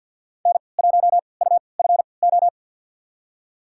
Morse code, heard as a keyed tone, about 46 wpm, spaced as for 27 wpm, 690 Hz. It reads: N1UFO
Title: USA Call Signs - 3x Character Spacing - 50wpm